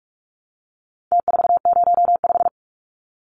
Morse code, heard as a keyed tone, about 46 wpm, 700 Hz. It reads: T405
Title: Numbers - Contest Style - 4-Digit 45wpm